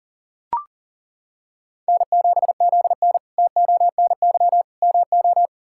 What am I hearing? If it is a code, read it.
Morse code, heard as a keyed tone, about 40 wpm, 700 Hz. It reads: N7ZN TONY MO
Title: K1USN SST QSO Practice - 40wpm